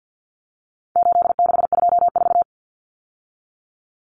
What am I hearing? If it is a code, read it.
Morse code, heard as a keyed tone, about 50 wpm, 700 Hz. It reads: 8624
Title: Common Cut Numbers - 4-Digit 50wpm